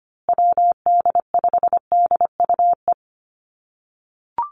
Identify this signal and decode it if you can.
Morse code, heard as a keyed tone, about 25 wpm, 700 Hz. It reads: WD5DUE
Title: USA Call Signs Repeated 3x 25wpm